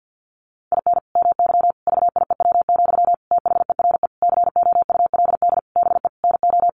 Morse code, heard as a keyed tone, about 50 wpm, 700 Hz. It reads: IN MY VIEW, THERE COULD BE NO
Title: Sentences from Top 1500 Words 50wpm